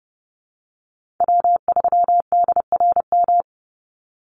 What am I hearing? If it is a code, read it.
Morse code, heard as a keyed tone, about 30 wpm, 700 Hz. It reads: W3DRM
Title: USA Call Signs 30wpm